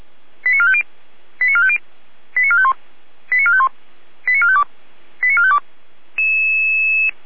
Five-tone selective call (ZVEI), heard as quick runs of five tones, tones of about 70 ms. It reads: 89450 89450 89451 89451 89452 89452